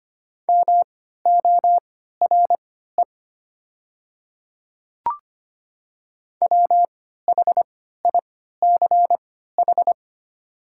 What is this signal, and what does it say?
Morse code, heard as a keyed tone, about 25 wpm, spaced as for 14 wpm, 700 Hz. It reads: MORE WHICH
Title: 3x Character Spacing - Top 100 Words - 25wpm